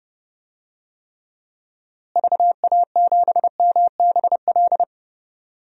Morse code, heard as a keyed tone, about 32 wpm, 700 Hz. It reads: VA7MBL
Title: DXCC-100 Call Signs - 6 Characters - Repeated 2x - 30wpm